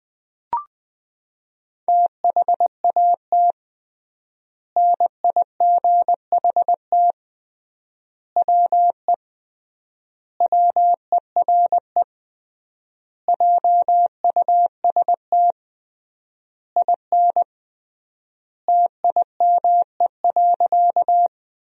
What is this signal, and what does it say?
Morse code, heard as a keyed tone, about 20 wpm, 700 Hz. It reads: THAT NIGHT WE WERE JUST IN TIME.
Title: 3x Word-Spacing - Sentences from Top 300 Words - 20wpm